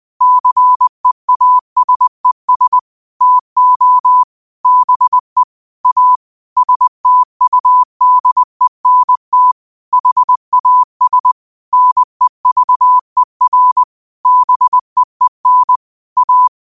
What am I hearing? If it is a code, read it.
Morse code, heard as a keyed tone, about 20 wpm, 1000 Hz. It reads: CEASES TO BE A STUDENT HAS NEVER BEEN A
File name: qotdfiles / 20wpm